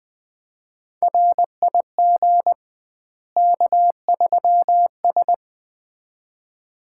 Morse code, heard as a keyed tone, about 20 wpm, 700 Hz. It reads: RIG K3S
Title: Common QSO Phrases - 2x Word-Spacing - 20wpm